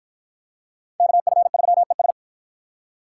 Morse code, heard as a keyed tone, about 53 wpm, 700 Hz. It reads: KF3ES